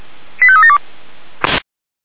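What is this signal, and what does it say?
Five-tone selective call (ZVEI), heard as quick runs of five tones, tones of about 70 ms. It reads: 85372
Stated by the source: ZVEI five-tone selective call received on SDR.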